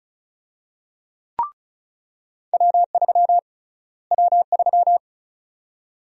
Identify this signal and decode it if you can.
Morse code, heard as a keyed tone, about 36 wpm, 700 Hz. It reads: W3 W3
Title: USA Call Sign Prefixes - Repeated 2x - 35wpm